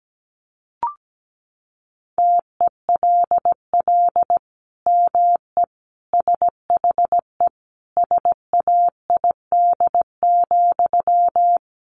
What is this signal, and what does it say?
Morse code, heard as a keyed tone, about 17 wpm, 700 Hz. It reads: TELL ME SHE SAID,
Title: Sentences from Top 1200 Words 17wpm